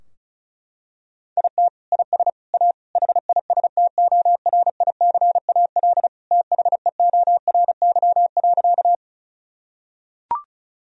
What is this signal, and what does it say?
Morse code, heard as a keyed tone, about 34 wpm, 700 Hz. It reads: IT IS A HISTORICAL THEORY.
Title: Sentences from Top 1000 Words - Review Part 4 - 35wpm